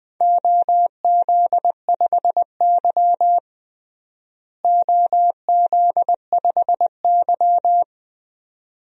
Morse code, heard as a keyed tone, about 20 wpm, 700 Hz. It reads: OZ5Y OZ5Y
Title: DXCC-100 Call Signs - 4 Characters - Repeated 2x - 20wpm